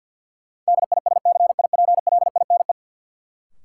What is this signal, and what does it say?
Morse code, heard as a keyed tone, about 50 wpm, 700 Hz. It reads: DISCIPLINE
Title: Sentences from Top 900 Words - New Word Drill - 50wpm